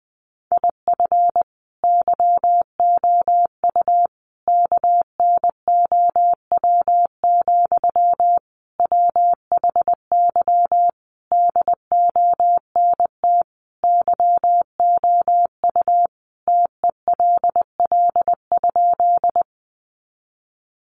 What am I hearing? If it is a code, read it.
Morse code, heard as a keyed tone, about 20 wpm, 700 Hz. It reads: IF YOU KNOW, WHY DONT YOU TELL?
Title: Sentences from Top 300 Words 20wpm